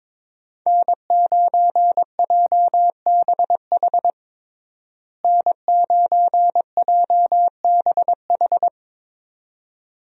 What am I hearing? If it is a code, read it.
Morse code, heard as a keyed tone, about 22 wpm, 700 Hz. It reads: N9JBH N9JBH
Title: USA Call Signs - 5 Characters - Repeated 2x - 22wpm